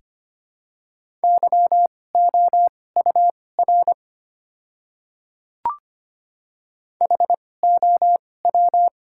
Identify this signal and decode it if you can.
Morse code, heard as a keyed tone, about 25 wpm, spaced as for 18 wpm, 690 Hz. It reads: YOUR HOW
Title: 2x Character Spacing - Top 200 Words - 25wpm